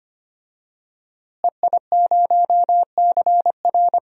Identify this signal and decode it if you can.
Morse code, heard as a keyed tone, about 25 wpm, 700 Hz. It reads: EI0CR
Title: DXCC-100 Call Signs - 5 Characters - Repeated 2x - 25wpm